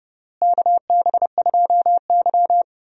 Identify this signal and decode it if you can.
Morse code, heard as a keyed tone, about 32 wpm, 700 Hz. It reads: KB2Y
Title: USA Call Signs - 4 Characters - Repeated 3x - 30wpm